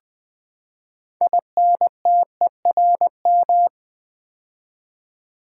Morse code, heard as a keyed tone, about 20 wpm, 700 Hz. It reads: INTERM
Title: Build A Word - 6 Letters At A Time - 12-Letter Words - 20wpm